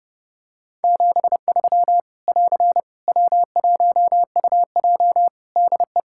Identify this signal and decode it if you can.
Morse code, heard as a keyed tone, about 32 wpm, 700 Hz. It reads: 73 <AR> W1UJ DE
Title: Practice QSO - Exchange 4 - Ending - 30wpm